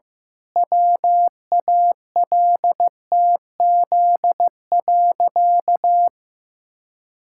Morse code, heard as a keyed tone, about 15 wpm, 700 Hz. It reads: WALTZ.
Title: Pangram Challenge v2 - 15wpm